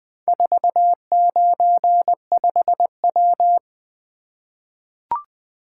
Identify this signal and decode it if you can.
Morse code, heard as a keyed tone, about 20 wpm, 700 Hz. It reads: 495W
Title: Four Letter-Number Combinations - 20wpm